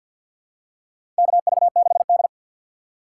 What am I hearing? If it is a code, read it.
Morse code, heard as a keyed tone, about 48 wpm, 700 Hz. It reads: KV6D
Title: USA Call Signs - 3x Character Spacing - 50wpm